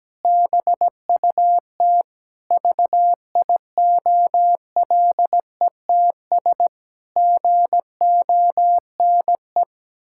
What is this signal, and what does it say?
Morse code, heard as a keyed tone, about 17 wpm, 700 Hz. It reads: BUT VIOLETS GONE
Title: Pangram Challenge v2 - 17wpm